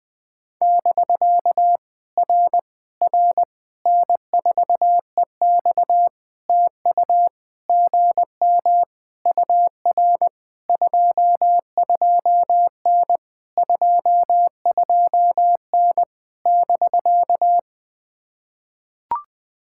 Morse code, heard as a keyed tone, about 19 wpm, 700 Hz. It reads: <BK> R R N4EX TU GM UR 22N 22N <BK>
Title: POTA QSO Practice - 20wpm